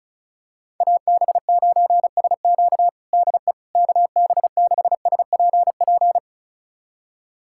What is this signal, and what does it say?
Morse code, heard as a keyed tone, about 36 wpm, 700 Hz. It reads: AB9SQ DE KB6SPP
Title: USA Callsign DE Callsign 35wpm